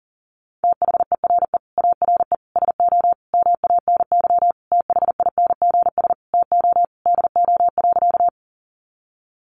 Morse code, heard as a keyed tone, about 40 wpm, 700 Hz. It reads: THERE ARE SO MANY THINGS TO DO.
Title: Additional Easy Phrases 40wpm